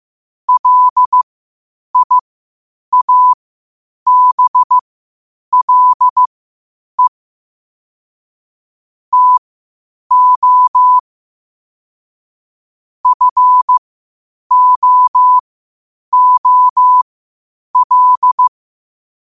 Morse code, heard as a keyed tone, about 15 wpm, spaced as for 8 wpm, 1000 Hz. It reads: LIABLE TO FOOL
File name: qotdfiles / 8wpm